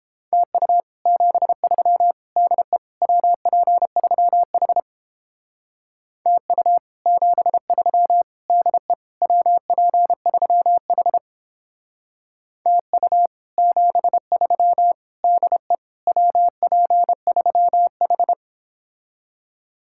Morse code, heard as a keyed tone, about 31 wpm, 700 Hz. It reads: TU 73 DE WP3H TU 73 DE WP3H TU 73 DE WP3H